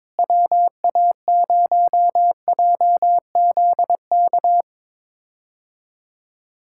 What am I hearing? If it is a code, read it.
Morse code, heard as a keyed tone, about 22 wpm, 700 Hz. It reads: WA0JZK
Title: K1USN SST QSO Practice - 22wpm